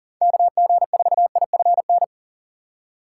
Morse code, heard as a keyed tone, about 40 wpm, 700 Hz. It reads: KG4IFN
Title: USA Call Signs - 4x Character Spacing - 40wpm